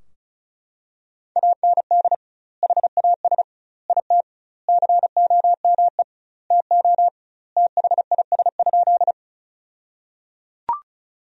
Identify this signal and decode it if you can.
Morse code, heard as a keyed tone, about 34 wpm, 700 Hz. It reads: AND HAS IT COME TO THIS?
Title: 2x Word-Spacing - Sentences from Top 100 Words - 35wpm